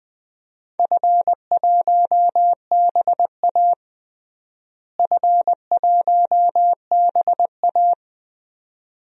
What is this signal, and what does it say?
Morse code, heard as a keyed tone, about 20 wpm, 700 Hz. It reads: F1BA F1BA
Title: DXCC-100 Call Signs - 4 Characters - Repeated 2x - 20wpm